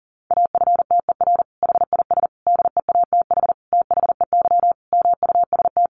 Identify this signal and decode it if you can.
Morse code, heard as a keyed tone, about 40 wpm, 700 Hz. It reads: AFTER HIS DEATH THEY MUST
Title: Sentences from Top 1100 Words 40wpm